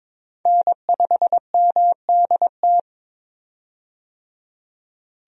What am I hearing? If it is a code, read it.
Morse code, heard as a keyed tone, about 21 wpm, 700 Hz. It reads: N5MDT
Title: USA Call Signs - 5 Characters - 22wpm